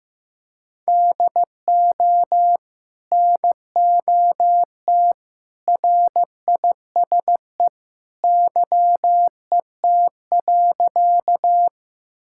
Morse code, heard as a keyed tone, about 15 wpm, 700 Hz. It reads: DO NOT RISE YET.
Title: Sentences from Top 1000 Words - Review Part 6 - 15wpm